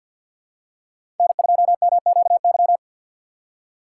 Morse code, heard as a keyed tone, about 51 wpm, 700 Hz. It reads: N2MQY
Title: USA Call Signs - 5 Characters - 2x Character Spacing - 50wpm